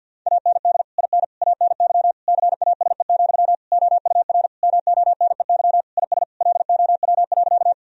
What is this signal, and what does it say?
Morse code, heard as a keyed tone, about 50 wpm, 700 Hz. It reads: AND IN ANY CASE, OUR MONEY IS LOW.
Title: Sentences from Top 1000 Words - Review Part 1 - 50wpm